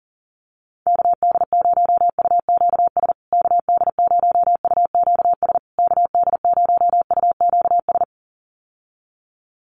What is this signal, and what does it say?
Morse code, heard as a keyed tone, about 40 wpm, 700 Hz. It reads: KD0UQS KD0UQS KD0UQS
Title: USA Call Signs Repeated 3x 40wpm